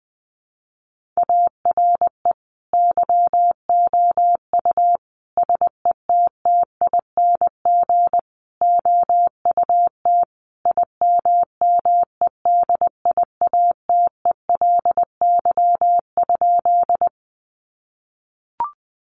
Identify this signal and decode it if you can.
Morse code, heard as a keyed tone, about 20 wpm, 700 Hz. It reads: ARE YOU SETTING OUT IMMEDIATELY?
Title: Sentences from Top 1300 Words 20wpm